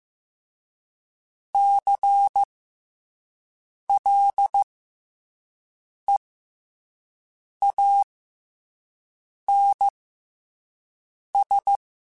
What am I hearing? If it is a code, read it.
Morse code, heard as a keyed tone, about 15 wpm, spaced as for 5 wpm, 800 Hz. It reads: CLEANS
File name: Special INGs-mp3s / Special INGs-mp3s / SlowToFaster ing @10